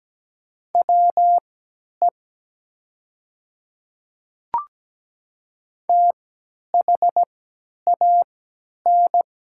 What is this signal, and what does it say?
Morse code, heard as a keyed tone, about 17 wpm, spaced as for 10 wpm, 700 Hz. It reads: WE THAN
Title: 3x Character Spacing - Top 100 Words - 17wpm